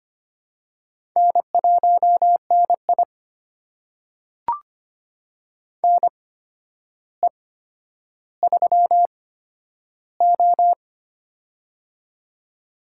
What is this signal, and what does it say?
Morse code, heard as a keyed tone, about 25 wpm, spaced as for 11 wpm, 700 Hz. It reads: N1NI NE3O
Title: USA Call Signs - 4 Characters - 8x Character Spacing - 25wpm